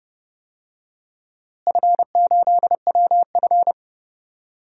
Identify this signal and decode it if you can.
Morse code, heard as a keyed tone, about 30 wpm, 700 Hz. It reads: F8WF
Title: DXCC-100 Call Signs - 4 Characters - Repeated 2x - 30wpm